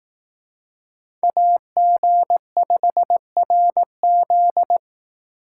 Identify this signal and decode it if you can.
Morse code, heard as a keyed tone, about 18 wpm, 700 Hz. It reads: AG5RZ
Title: Speed Racing - USA 2x2 Call Signs 15wpm